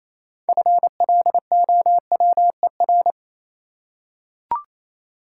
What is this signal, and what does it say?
Morse code, heard as a keyed tone, about 28 wpm, 700 Hz. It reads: FLOWER
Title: Six Letter Words - 28wpm